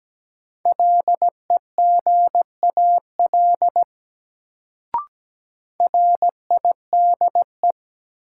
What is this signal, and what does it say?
Morse code, heard as a keyed tone, about 17 wpm, 700 Hz. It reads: LEGAL RIDE
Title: Sentences from Top 1200 Words - New Word Drill Alt - 17wpm